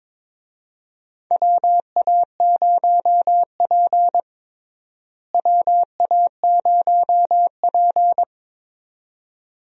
Morse code, heard as a keyed tone, about 22 wpm, 700 Hz. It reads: WA0P WA0P
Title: USA Call Signs - 4 Characters - Repeated 2x - 22wpm